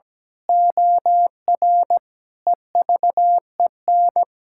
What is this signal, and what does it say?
Morse code, heard as a keyed tone, about 17 wpm, 700 Hz. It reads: OR EVEN
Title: Pangram Challenge v2 - 17wpm